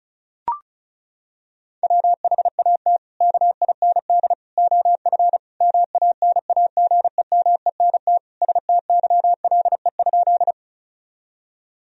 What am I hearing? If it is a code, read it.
Morse code, heard as a keyed tone, about 36 wpm, 700 Hz. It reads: WHAT KIND OF MANAGEMENT STYLE?
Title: Sentences from Top 1000 Words - Review Part 4 - 35wpm